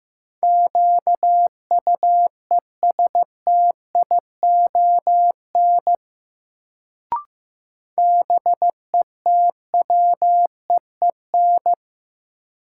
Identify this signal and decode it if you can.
Morse code, heard as a keyed tone, about 15 wpm, 700 Hz. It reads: QUESTION BETWEEN
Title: Sentences from Top 400 Words - New Word Drill - 15wpm